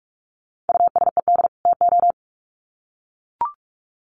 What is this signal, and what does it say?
Morse code, heard as a keyed tone, about 44 wpm, 700 Hz. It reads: USED TO
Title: Sets of 2 Words 45wpm